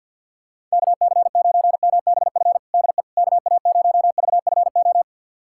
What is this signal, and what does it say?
Morse code, heard as a keyed tone, about 51 wpm, 700 Hz. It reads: KC9MBF DE KA0VFO
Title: USA Callsign DE Callsign - 6 Characters - 50wpm